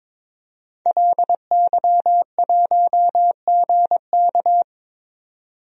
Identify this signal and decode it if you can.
Morse code, heard as a keyed tone, about 22 wpm, 700 Hz. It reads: LY1GK
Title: DXCC-100 Call Signs - 5 Characters - 22wpm